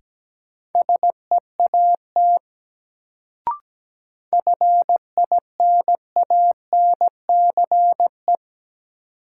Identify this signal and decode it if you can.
Morse code, heard as a keyed tone, about 17 wpm, 700 Hz. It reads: SEAT FINANCE
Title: Sentences from Top 1500 Words - New Word Drill - 17wpm